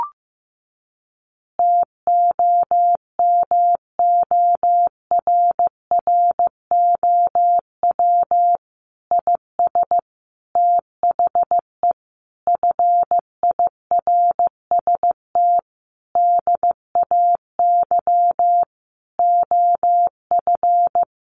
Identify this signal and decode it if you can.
Morse code, heard as a keyed tone, about 15 wpm, 700 Hz. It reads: TOMORROW IS THE FIRST DAY OF
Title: Additional Easy Phrases 15wpm